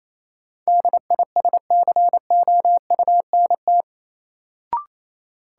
Morse code, heard as a keyed tone, about 28 wpm, 700 Hz. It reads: DISCOUNT
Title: Sentences from Top 1200 Words - New Word Drill - 28wpm